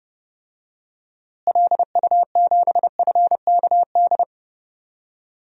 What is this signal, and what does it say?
Morse code, heard as a keyed tone, about 30 wpm, 700 Hz. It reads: LU7FKD
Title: DXCC-100 Call Signs - 6 Characters - Repeated 2x - 30wpm